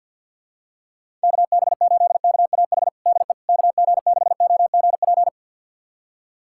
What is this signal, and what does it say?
Morse code, heard as a keyed tone, about 51 wpm, 700 Hz. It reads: KB8KAH DE KG6OGP
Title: USA Callsign DE Callsign - 6 Characters - 50wpm